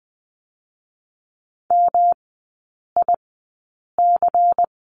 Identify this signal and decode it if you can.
Morse code, heard as a keyed tone, about 20 wpm, spaced as for 8 wpm, 700 Hz. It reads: MIC
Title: 152 Lesson 25e - Letter p - Three Characters 20wpm